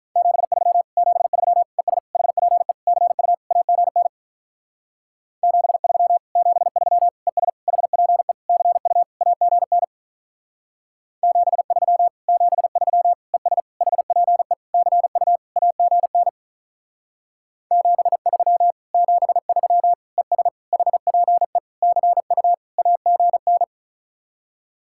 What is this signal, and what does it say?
Morse code, heard as a keyed tone, about 46 wpm, 700 Hz. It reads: 73 73 ES HPE CU AGN 73 73 ES HPE CU AGN 73 73 ES HPE CU AGN 73 73 ES HPE CU AGN